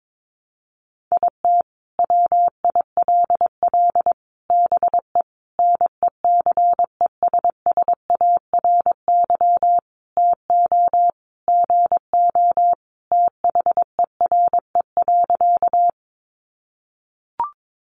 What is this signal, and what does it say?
Morse code, heard as a keyed tone, about 22 wpm, 700 Hz. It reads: IT WILL BE NECESSARY TO GO THERE.
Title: Sentences from Top 800 Words 22wpm